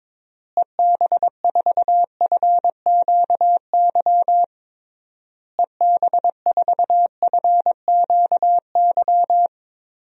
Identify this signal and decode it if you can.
Morse code, heard as a keyed tone, about 22 wpm, 700 Hz. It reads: EB4FQY EB4FQY
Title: DXCC-100 Call Signs - 6 Characters - Repeated 3x - 22wpm